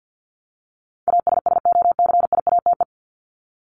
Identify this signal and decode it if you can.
Morse code, heard as a keyed tone, about 50 wpm, 700 Hz. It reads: ASSOCIATE
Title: ED Suffix - Top 100 Words with Base Words 50wpm